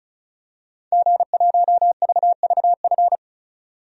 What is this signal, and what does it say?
Morse code, heard as a keyed tone, about 34 wpm, 700 Hz. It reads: G1VVF
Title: DXCC-100 Call Signs - 5 Characters - 35wpm